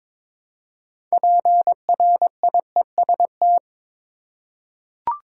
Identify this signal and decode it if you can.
Morse code, heard as a keyed tone, about 22 wpm, 700 Hz. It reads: PRIEST
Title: Six Letter Words - 22wpm